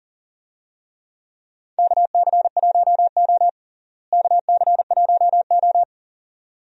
Morse code, heard as a keyed tone, about 40 wpm, 700 Hz. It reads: KC1O KC1O
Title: USA Call Signs - 4 Characters - Repeated 2x - 40wpm